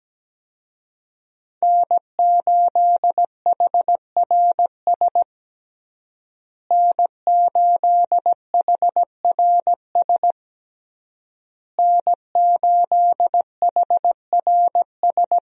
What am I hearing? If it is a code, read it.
Morse code, heard as a keyed tone, about 17 wpm, 700 Hz. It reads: N8HRS N8HRS N8HRS
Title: USA Call Signs - 5 Characters - Repeated 3x - 17wpm